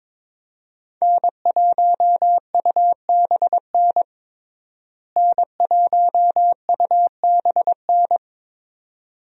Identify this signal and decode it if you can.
Morse code, heard as a keyed tone, about 22 wpm, 700 Hz. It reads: N1UBN N1UBN
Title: USA Call Signs - 5 Characters - Repeated 2x - 22wpm